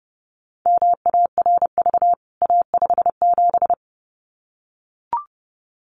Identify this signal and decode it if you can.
Morse code, heard as a keyed tone, about 30 wpm, 700 Hz. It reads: MARV A57
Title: CWT Exchange 30wpm